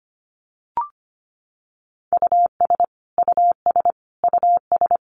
Morse code, heard as a keyed tone, about 25 wpm, 700 Hz. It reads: US US US
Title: Top 500 Words Repeated 25wpm